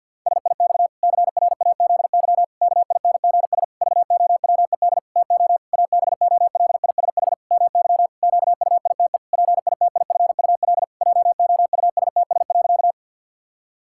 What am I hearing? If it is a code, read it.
Morse code, heard as a keyed tone, about 50 wpm, 700 Hz. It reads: SIX CRAZY KINGS VOWED TO ABOLISH MY QUITE PITIFUL JOUSTS.
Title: Pangram Challenge v2 - 50wpm